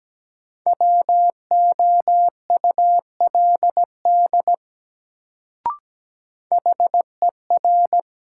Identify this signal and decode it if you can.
Morse code, heard as a keyed tone, about 17 wpm, 700 Hz. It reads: WOULD HER
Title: Sentences from Top 100 Words - New Word Drill - 17wpm